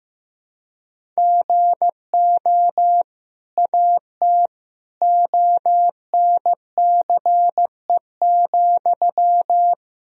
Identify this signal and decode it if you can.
Morse code, heard as a keyed tone, about 15 wpm, 700 Hz. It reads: GO AT ONCE,
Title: Sentences from Top 1000 Words - Review Part 1 - 15wpm